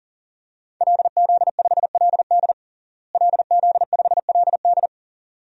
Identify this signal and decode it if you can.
Morse code, heard as a keyed tone, about 40 wpm, 700 Hz. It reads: LZ5LD LZ5LD
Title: DXCC-100 Call Signs - 5 Characters - Repeated 2x - 40wpm